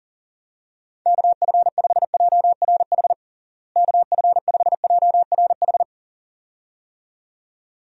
Morse code, heard as a keyed tone, about 40 wpm, 700 Hz. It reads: KF5JRH KF5JRH
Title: USA Call Signs - 6 Characters - Repeated 2x - 40wpm